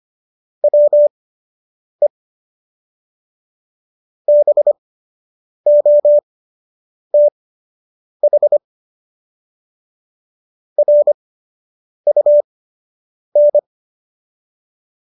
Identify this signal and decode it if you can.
Morse code, heard as a keyed tone, about 25 wpm, spaced as for 8 wpm, 590 Hz. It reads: WE BOTH RUN
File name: qso006 08